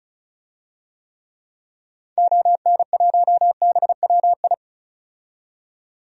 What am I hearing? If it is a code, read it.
Morse code, heard as a keyed tone, about 34 wpm, 700 Hz. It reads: ON1BWI
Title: DXCC-100 Call Signs - 6 Characters - 35wpm